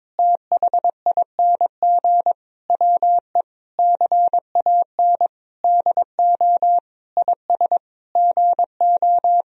Morse code, heard as a keyed tone, about 22 wpm, 700 Hz. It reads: THING WE CAN DO IS GO
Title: Sentences from Top 200 Words and Abbreviations 22wpm